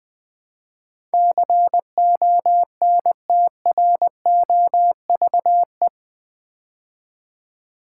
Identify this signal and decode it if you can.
Morse code, heard as a keyed tone, about 20 wpm, 700 Hz. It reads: CONTROVE
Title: Build A Word - 4 Letters At A Time - 11-Letter Words - 20wpm